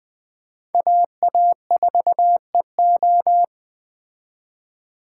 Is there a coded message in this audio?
Yes: Morse code from 0.7 to 3.4 s, heard as a keyed tone, about 20 wpm, 700 Hz. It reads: AA4EO